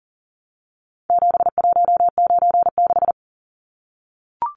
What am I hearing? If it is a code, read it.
Morse code, heard as a keyed tone, about 40 wpm, 700 Hz. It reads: 7196
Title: Numbers - 4-Digit 40wpm